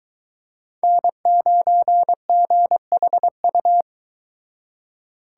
Morse code, heard as a keyed tone, about 23 wpm, 700 Hz. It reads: N9GHU